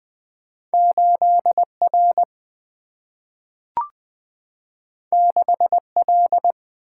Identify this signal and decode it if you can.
Morse code, heard as a keyed tone, about 20 wpm, 700 Hz. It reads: 8 R 6 L
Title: Single-Letter Single-Number 20wpm